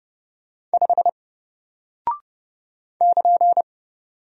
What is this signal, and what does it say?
Morse code, heard as a keyed tone, about 30 wpm, 700 Hz. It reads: 5 <KN>
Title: Single Character (Letter:Number:Punctuation:Prosign) Alt 30wpm